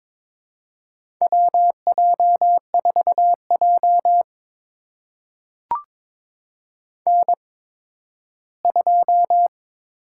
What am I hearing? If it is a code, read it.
Morse code, heard as a keyed tone, about 21 wpm, spaced as for 13 wpm, 700 Hz. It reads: WJ4J N2
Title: USA Call Signs - 8x Character Spacing - 22wpm